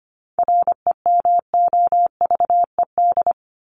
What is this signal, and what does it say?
Morse code, heard as a keyed tone, about 25 wpm, 700 Hz. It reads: REMOVED
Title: ED Suffix - Top 100 Words with Base Words 25wpm